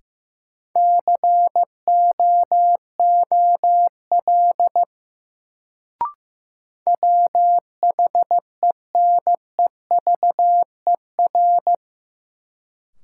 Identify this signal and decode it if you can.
Morse code, heard as a keyed tone, about 15 wpm, 700 Hz. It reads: COOL WHENEVER
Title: Sentences from Top 1200 Words - New Word Drill Alt - 15wpm